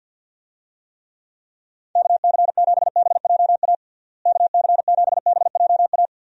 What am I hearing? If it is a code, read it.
Morse code, heard as a keyed tone, about 50 wpm, 700 Hz. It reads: KC7BJA KC7BJA
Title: USA Call Signs - 6 Characters - Repeated 3x - 50wpm